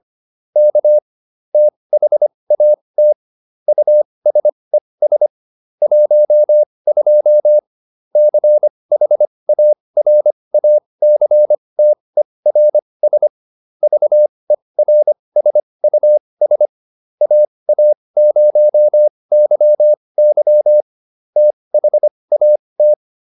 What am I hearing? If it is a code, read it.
Morse code, heard as a keyed tone, about 25 wpm, spaced as for 20 wpm, 600 Hz. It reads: K THAT USES 12 CHARACTERS VERSUS AA0YY THAT